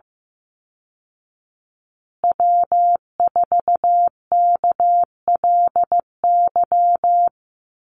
Morse code, heard as a keyed tone, about 14 wpm, 700 Hz. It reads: W4KLY
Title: Call Signs Repeated 2x - 15wpm